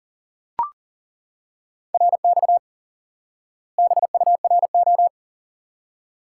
Morse code, heard as a keyed tone, about 40 wpm, 700 Hz. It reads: RX BURO
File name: IQE Course 3 - Lesson 89 - YAGI - Two Elements - 40wpm